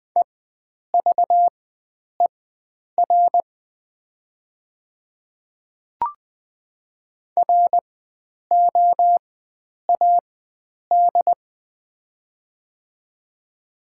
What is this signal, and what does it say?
Morse code, heard as a keyed tone, about 20 wpm, spaced as for 9 wpm, 700 Hz. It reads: EVER ROAD
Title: 4x Character Spacing - Top 500 Words - 20wpm